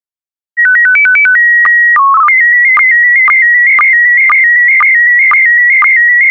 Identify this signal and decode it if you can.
SSTV (slow-scan television), PD 120.